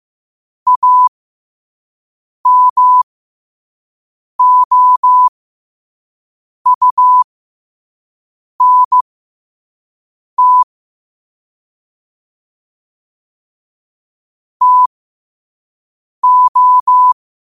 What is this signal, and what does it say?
Morse code, heard as a keyed tone, about 15 wpm, spaced as for 5 wpm, 1000 Hz. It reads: AMOUNT TO
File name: qotdfiles / 5wpm